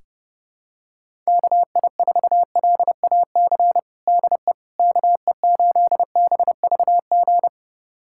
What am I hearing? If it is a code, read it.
Morse code, heard as a keyed tone, about 32 wpm, 700 Hz. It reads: KI4LAC DE KE8BVG
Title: USA Callsign DE Callsign - 6 Characters - 30wpm